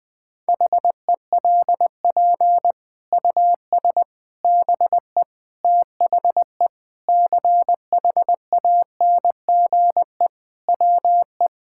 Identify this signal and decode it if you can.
Morse code, heard as a keyed tone, about 20 wpm, 700 Hz. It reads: HELP US BE THE CHANGE WE
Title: Sentences with 8-Word Phrases - Spoken Beginning - 20wpm